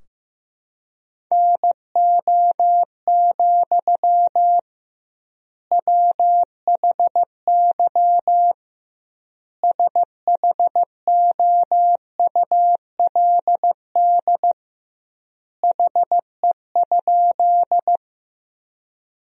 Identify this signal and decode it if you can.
Morse code, heard as a keyed tone, about 16 wpm, 700 Hz. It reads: NO, WHY SHOULD HE?
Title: 2x Word-Spacing - Sentences from Top 300 Words - 15wpm